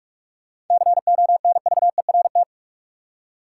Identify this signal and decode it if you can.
Morse code, heard as a keyed tone, about 43 wpm, 700 Hz. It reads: CONVERT
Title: Seven Letter Words - 45wpm